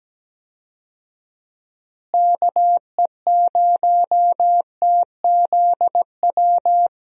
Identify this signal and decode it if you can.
Morse code, heard as a keyed tone, about 17 wpm, 700 Hz. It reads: KE0TZW
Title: USA Call Signs - 6 Characters - Repeated 3x - 17wpm